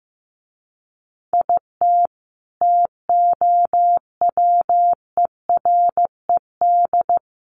Morse code, heard as a keyed tone, about 15 wpm, 700 Hz. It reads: IT TOWERED
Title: A Sound Of Thunder 15wpm